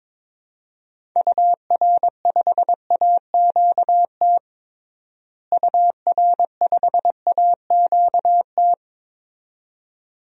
Morse code, heard as a keyed tone, about 22 wpm, 700 Hz. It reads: UR5AQT UR5AQT
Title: DXCC-100 Call Signs - 6 Characters - Repeated 2x - 22wpm